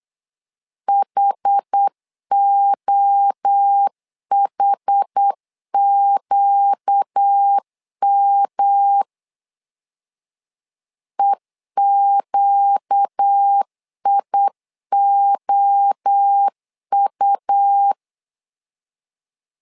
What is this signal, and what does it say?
Morse code, heard as a keyed tone, about 8 wpm, 800 Hz. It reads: HOHQM EQIOU
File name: 4 Q 8wpm